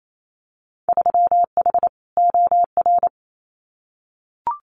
Morse code, heard as a keyed tone, about 28 wpm, 700 Hz. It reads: 3H OR
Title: Winter Field Day Exchanges Encore 28wpm